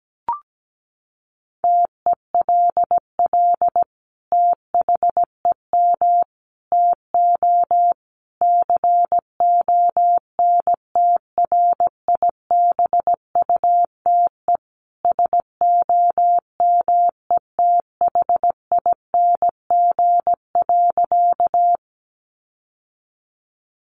Morse code, heard as a keyed tone, about 17 wpm, 700 Hz. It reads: TELL THEM TO CONTRIBUTE SOMETHING.
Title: Sentences from Top 1400 Words 17wpm